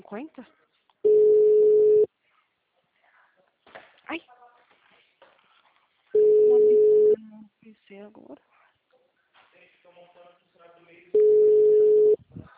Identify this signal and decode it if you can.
Ringback tone (Europe, 425 Hz), with speech.
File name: out- -2003- - - .1485